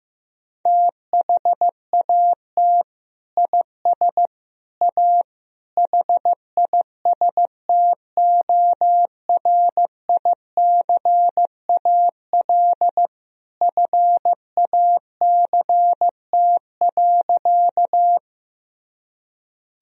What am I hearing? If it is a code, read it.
Morse code, heard as a keyed tone, about 15 wpm, 700 Hz. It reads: THAT IS A HISTORICAL FACT.
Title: Sentences from Top 1000 Words - Review Part 4 - 15wpm